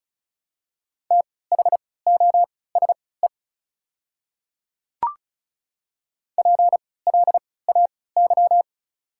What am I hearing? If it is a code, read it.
Morse code, heard as a keyed tone, about 36 wpm, spaced as for 20 wpm, 700 Hz. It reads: THOSE PLAY
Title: 3x Character Spacing - Top 500 Words - 35wpm